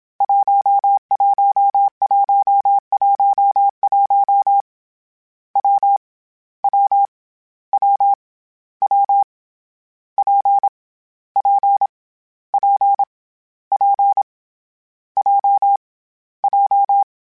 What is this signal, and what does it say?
Morse code, heard as a keyed tone, about 27 wpm, 800 Hz. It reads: 11111 W W W W P P P P J J
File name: Difficult Combos / Difficult Combos-mp3s / WPJ WPJ1 @25